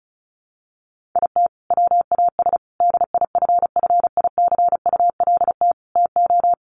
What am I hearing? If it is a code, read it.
Morse code, heard as a keyed tone, about 34 wpm, 700 Hz. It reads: IT WAS DIFFICULT TO
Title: Sentences from Top 1400 Words 35wpm